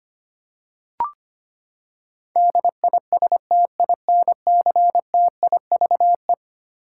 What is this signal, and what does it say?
Morse code, heard as a keyed tone, about 25 wpm, 700 Hz. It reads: DISTINCTIVE
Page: morse-code-ninja-practice-25wpm